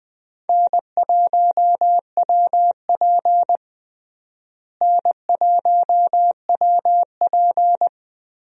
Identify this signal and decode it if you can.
Morse code, heard as a keyed tone, about 20 wpm, 700 Hz. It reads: N1WP N1WP
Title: USA Call Signs - 4 Characters - Repeated 3x - 20wpm